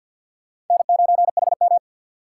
Morse code, heard as a keyed tone, about 51 wpm, 700 Hz. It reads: N9HM
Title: USA Call Signs - 4 Characters - 2x Character Spacing - 50wpm